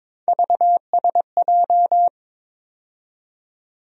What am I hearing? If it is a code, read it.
Morse code, heard as a keyed tone, about 21 wpm, 700 Hz. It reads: VSJ
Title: USA Call Sign Postfixes - 3 Letters 22wpm